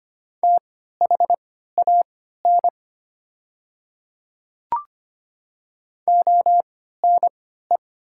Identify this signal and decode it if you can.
Morse code, heard as a keyed tone, about 25 wpm, spaced as for 14 wpm, 700 Hz. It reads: THAN ONE
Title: 3x Character Spacing - Top 100 Words - 25wpm